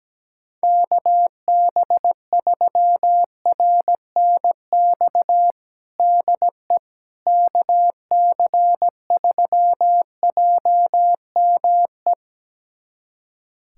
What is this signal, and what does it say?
Morse code, heard as a keyed tone, about 17 wpm, 700 Hz. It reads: KB3RNX DE KC3JME
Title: USA Callsign DE Callsign 17wpm